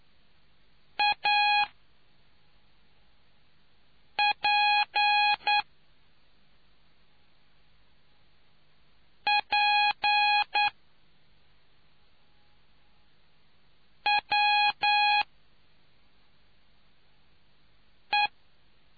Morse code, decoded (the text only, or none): APPWE